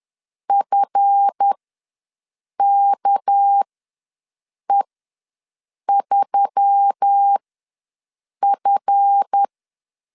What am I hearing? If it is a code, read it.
Morse code, heard as a keyed tone, about 10 wpm, spaced as for 6 wpm, 800 Hz. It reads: FKE3F
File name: LEZIONI / 7 GF9Z / 7 GF ws 10wpm 3918